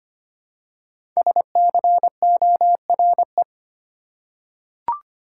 Morse code, heard as a keyed tone, about 25 wpm, 700 Hz. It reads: SCORE
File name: 5 Letter Words - First 3 Letters Given - 25wpm